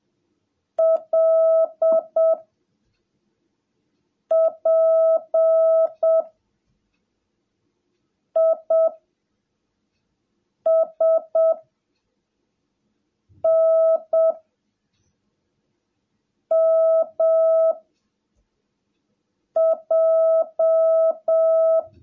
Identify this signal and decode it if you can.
Morse code, heard as a keyed tone, about 7 wpm, 650 Hz. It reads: L P I S N M J
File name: jh8cbh / 2021cw / cw17a